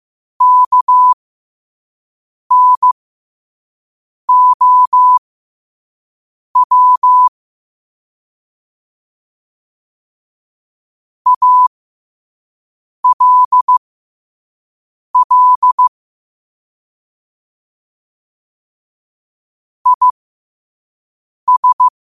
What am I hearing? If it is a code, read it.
Morse code, heard as a keyed tone, about 15 wpm, spaced as for 5 wpm, 1000 Hz. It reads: KNOW ALL IS
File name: qotdfiles / 5wpm